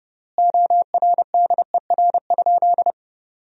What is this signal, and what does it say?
Morse code, heard as a keyed tone, about 30 wpm, 700 Hz. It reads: ORDER?
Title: 3x Word-Spacing - Sentences from Top 400 Words - 30wpm